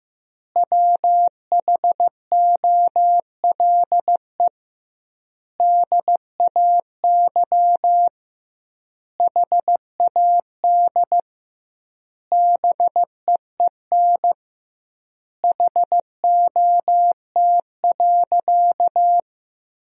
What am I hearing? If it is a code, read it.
Morse code, heard as a keyed tone, about 15 wpm, 700 Hz. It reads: WHOLE DAY HAD BEEN HOT.
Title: 2x Word-Spacing - Sentences from Top 500 Words - 15wpm